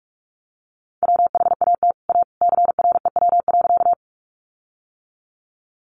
Morse code, heard as a keyed tone, about 44 wpm, 700 Hz. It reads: WHAT A CREW.
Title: Sentences from Top 1500 Words 45wpm